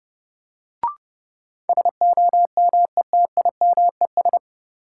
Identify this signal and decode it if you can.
Morse code, heard as a keyed tone, about 32 wpm, 700 Hz. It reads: SOMETIMES
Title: Sentences from Top 600 Words - New Word Drill - 30wpm